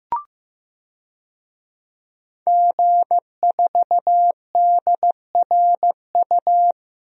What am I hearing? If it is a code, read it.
Morse code, heard as a keyed tone, about 15 wpm, 700 Hz. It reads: G4DRU
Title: DXCC-100 Call Signs - 5 Characters - Repeated 3x - 15wpm